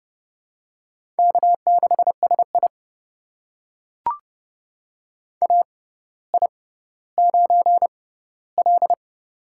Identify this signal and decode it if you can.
Morse code, heard as a keyed tone, about 30 wpm, spaced as for 15 wpm, 700 Hz. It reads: K6SI AI9L
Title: USA Call Signs - 4 Characters - 6x Character Spacing - 30wpm